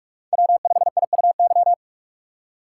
Morse code, heard as a keyed tone, about 42 wpm, 700 Hz. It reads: W5IUY